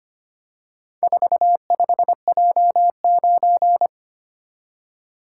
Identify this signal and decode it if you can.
Morse code, heard as a keyed tone, about 25 wpm, 700 Hz. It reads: 45J9
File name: practice / 25wpm / Four Letter-Number Combinations - 25wpm